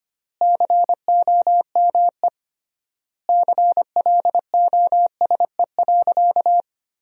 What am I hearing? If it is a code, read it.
Morse code, heard as a keyed tone, about 25 wpm, 700 Hz. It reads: COME CLOSE.
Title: 3x Word-Spacing - Sentences from Top 300 Words - 25wpm